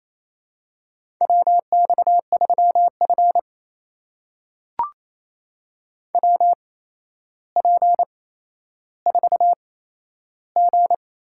Morse code, heard as a keyed tone, about 27 wpm, spaced as for 12 wpm, 700 Hz. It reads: WX3F WP4G
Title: USA Call Signs - 4 Characters - 8x Character Spacing - 28wpm